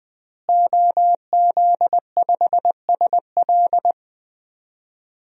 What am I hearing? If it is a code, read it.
Morse code, heard as a keyed tone, about 20 wpm, 700 Hz. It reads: OZ5SL
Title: DXCC-100 Call Signs - 5 Characters - Repeated 2x - 20wpm